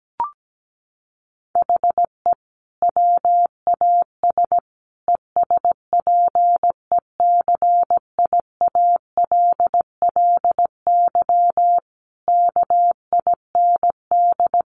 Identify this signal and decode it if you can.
Morse code, heard as a keyed tone, about 17 wpm, 700 Hz. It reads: HE WAS ESPECIALLY KIND
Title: Sentences from Top 700 Words 17wpm